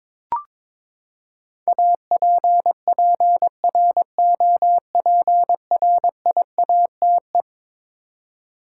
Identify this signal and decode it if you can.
Morse code, heard as a keyed tone, about 22 wpm, 700 Hz. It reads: APPROPRIATE
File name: practice / 22wpm / Eleven Letter Words - 22wpm